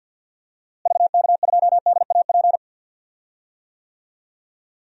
Morse code, heard as a keyed tone, about 46 wpm, 700 Hz. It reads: VK2DAP
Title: DXCC-100 Call Signs - 6 Characters - Repeated 3x - 50wpm